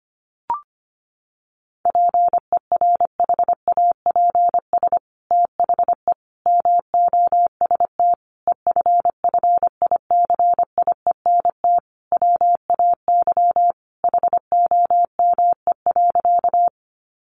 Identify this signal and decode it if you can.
Morse code, heard as a keyed tone, about 25 wpm, 700 Hz. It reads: PERHAPS THE MOST EFFICIENT WAY HOME.
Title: Sentences from Top 1300 Words 25wpm